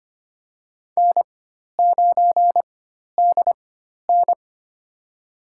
Morse code, heard as a keyed tone, about 25 wpm, spaced as for 12 wpm, 700 Hz. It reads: N9DN
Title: USA Call Signs - 4 Characters - 4x Character Spacing - 25wpm